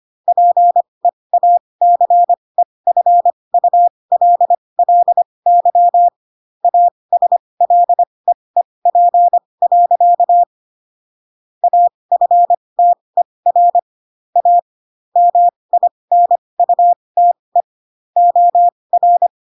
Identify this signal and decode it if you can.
Morse code, heard as a keyed tone, about 25 wpm, spaced as for 20 wpm, 700 Hz. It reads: PEACEFULLY ASLEEP. AFTER A MINUTE OR
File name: Pinocchio / Pinocchio ch6 15 25 ik1hge